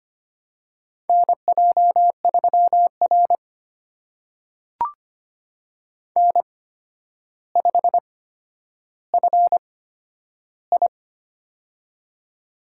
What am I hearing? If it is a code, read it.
Morse code, heard as a keyed tone, about 25 wpm, spaced as for 11 wpm, 700 Hz. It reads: NJ3R N5FI
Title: USA Call Signs - 4 Characters - 8x Character Spacing - 25wpm